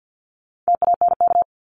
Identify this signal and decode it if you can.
Morse code, heard as a keyed tone, about 50 wpm, 700 Hz. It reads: TANK
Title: Top 1751-2000 Words 50wpm